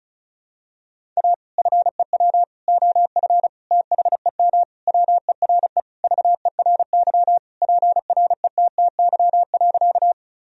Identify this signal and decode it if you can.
Morse code, heard as a keyed tone, about 36 wpm, 700 Hz. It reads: A FEW OF THEM WERE VERY PRETTY.
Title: Sentences from Top 1000 Words - Review Part 1 - 35wpm